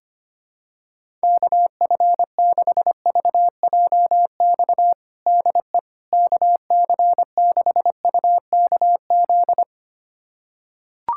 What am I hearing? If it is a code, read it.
Morse code, heard as a keyed tone, about 25 wpm, 700 Hz. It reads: KF6VJX DE KC6UKZ
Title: USA Callsign DE Callsign - 6 Characters - 25wpm